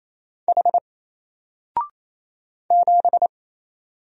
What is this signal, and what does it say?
Morse code, heard as a keyed tone, about 27 wpm, 700 Hz. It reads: H 7
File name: Single-Letter-Number-28wpm